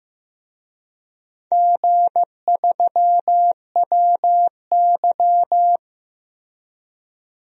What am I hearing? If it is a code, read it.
Morse code, heard as a keyed tone, about 15 wpm, 700 Hz. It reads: G3WY
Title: DXCC-100 Call Signs - 4 Characters - Repeated 3x - 15wpm